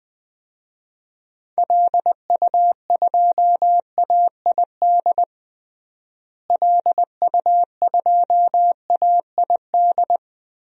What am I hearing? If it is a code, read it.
Morse code, heard as a keyed tone, about 20 wpm, 700 Hz. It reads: LU2AID LU2AID
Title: DXCC-100 Call Signs - 6 Characters - Repeated 3x - 20wpm